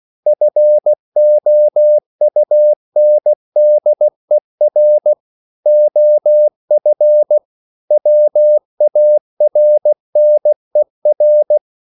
Morse code, heard as a keyed tone, about 16 wpm, 600 Hz. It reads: FOUNDER OF WARNER
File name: CW / Practice / EnglishQuotations0416